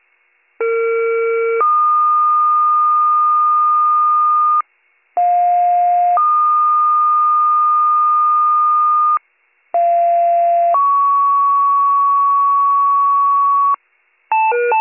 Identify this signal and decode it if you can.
Two-tone page, heard as a one-second tone, then a three-second tone, about 460 then 1220 Hz, about 710 then 1220 Hz, about 690 then 1120 Hz. It is heard from 0.6 to 13.7 s.